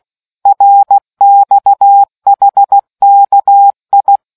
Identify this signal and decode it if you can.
Morse code, heard as a keyed tone, about 16 wpm, 800 Hz. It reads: RXHKI